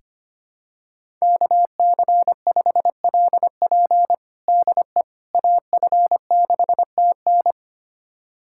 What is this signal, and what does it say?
Morse code, heard as a keyed tone, about 25 wpm, 700 Hz. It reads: KC5LP DE AF6TN
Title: USA Callsign DE Callsign - 5 Characters - 25wpm